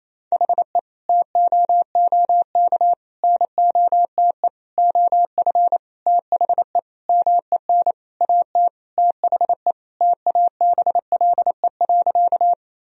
Morse code, heard as a keyed tone, about 28 wpm, 700 Hz. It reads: HE TOOK NOTE OF THE MEN AT THE TABLE.
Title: Sentences from Top 1000 Words - Review Part 2 - 28wpm